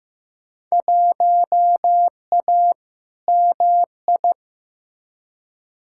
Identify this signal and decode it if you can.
Morse code, heard as a keyed tone, about 15 wpm, 700 Hz. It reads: 1A MI
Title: Field Day Exchanges - 15wpm